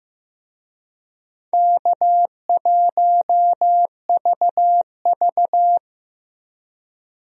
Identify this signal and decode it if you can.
Morse code, heard as a keyed tone, about 15 wpm, 700 Hz. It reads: K1VV
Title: USA Call Signs - 4 Characters - Repeated 3x - 15wpm